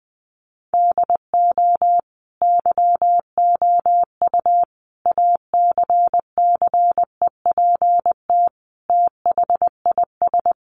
Morse code, heard as a keyed tone, about 20 wpm, 700 Hz. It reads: DO YOU ACCEPT THIS
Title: Sentences from Top 1300 Words 20wpm